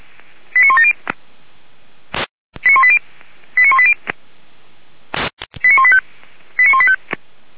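Five-tone selective call (ZVEI), heard as quick runs of five tones, tones of about 70 ms. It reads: 89170 89180 89180 89185 89185
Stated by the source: Five-tone ZVEI selective call received by SDR, from audio.f85360.feuerwehr-orlamuende.org fire brigade server.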